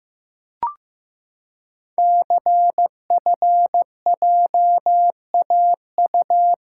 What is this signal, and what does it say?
Morse code, heard as a keyed tone, about 15 wpm, 700 Hz. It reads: CFJAU
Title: Code Groups - 5 Letters - 15wpm